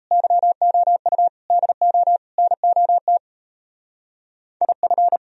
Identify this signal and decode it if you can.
Morse code, heard as a keyed tone, about 34 wpm, 700 Hz. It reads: YOU DO NOT IF
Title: Speed-Racing - Sets of 4 Words 28wpm